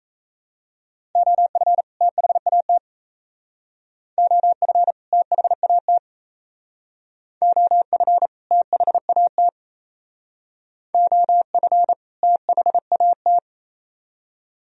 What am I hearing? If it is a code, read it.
Morse code, heard as a keyed tone, about 37 wpm, 700 Hz. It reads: OF THAT OF THAT OF THAT OF THAT